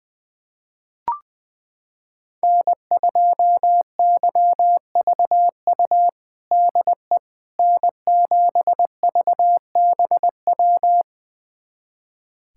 Morse code, heard as a keyed tone, about 20 wpm, 700 Hz. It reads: N2YVU DE N7VBW
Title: USA Callsign DE Callsign 20wpm